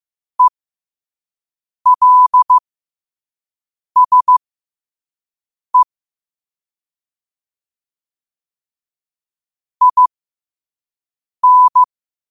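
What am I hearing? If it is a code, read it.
Morse code, heard as a keyed tone, about 15 wpm, spaced as for 5 wpm, 1000 Hz. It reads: ELSE IN
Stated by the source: Morse code quote of the day sent at 5 wpm.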